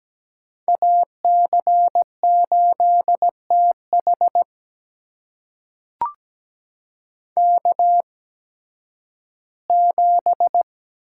Morse code, heard as a keyed tone, about 17 wpm, spaced as for 11 wpm, 700 Hz. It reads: AC8TH K7
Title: USA Call Signs - 5 Characters - 8x Character Spacing - 17wpm